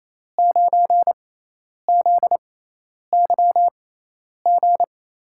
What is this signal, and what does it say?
Morse code, heard as a keyed tone, about 28 wpm, spaced as for 10 wpm, 700 Hz. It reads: 9ZYG
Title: USA Call Signs - 5 Characters - 6x Character Spacing - 28wpm